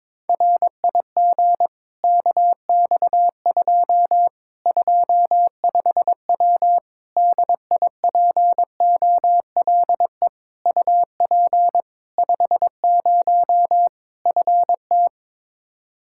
Morse code, heard as a keyed tone, about 22 wpm, 700 Hz. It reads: RIG KX2 25W DIPOLE UP 50 FT
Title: Practice QSO - Exchange 2 - Info - 22wpm